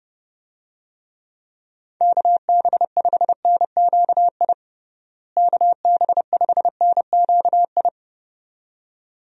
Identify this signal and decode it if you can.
Morse code, heard as a keyed tone, about 30 wpm, 700 Hz. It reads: KB5NQI KB5NQI
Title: USA Call Signs - 6 Characters - Repeated 2x - 30wpm